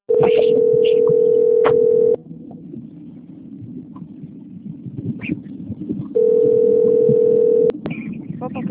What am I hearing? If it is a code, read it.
Ringback tone (North America), 440 + 480 Hz, with speech.
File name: out- - - - - .14327